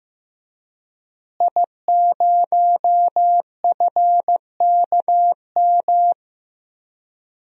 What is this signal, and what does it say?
Morse code, heard as a keyed tone, about 15 wpm, 700 Hz. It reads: I0FKM
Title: DXCC-100 Call Signs - 5 Characters - Repeated 3x - 15wpm